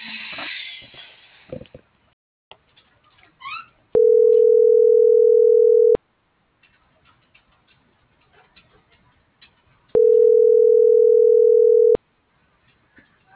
Ringback tone (North America), 440 + 480 Hz.